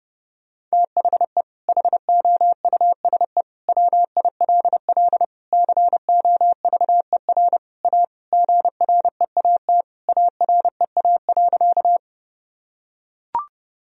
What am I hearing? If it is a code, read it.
Morse code, heard as a keyed tone, about 30 wpm, 700 Hz. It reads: THE HOUSE WILL COVER A GREAT AREA.
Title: Sentences from Top 1000 Words - Review Part 2 - 30wpm